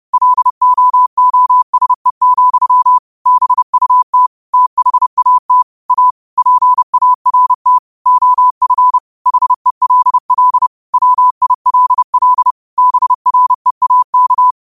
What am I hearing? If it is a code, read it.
Morse code, heard as a keyed tone, about 30 wpm, 1000 Hz. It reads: LOOSE, BUT THAT A PART OF HELL WILL BREAK
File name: qotdfiles / 30wpm